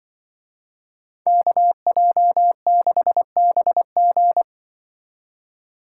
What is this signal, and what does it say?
Morse code, heard as a keyed tone, about 24 wpm, 700 Hz. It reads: KJ6BG
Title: Speed Racing - USA 2x2 Call Signs 20wpm